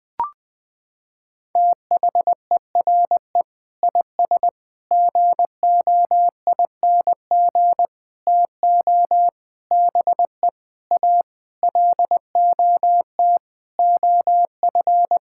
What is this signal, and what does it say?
Morse code, heard as a keyed tone, about 21 wpm, 700 Hz. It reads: THERE IS GOING TO BE A LOT OF
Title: Sentences with 7-Word Phrases - 20wpm